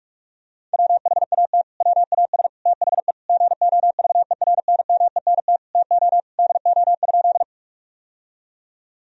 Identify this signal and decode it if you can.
Morse code, heard as a keyed tone, about 41 wpm, 700 Hz. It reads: WHAT WAS THE GOVERNMENT TO DO?
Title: Sentences from Top 1000 Words - Review Part 3 - 45wpm